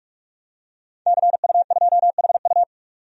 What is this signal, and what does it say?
Morse code, heard as a keyed tone, about 46 wpm, 700 Hz. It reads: CU2HU
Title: DXCC-100 Call Signs - 5 Characters - 45wpm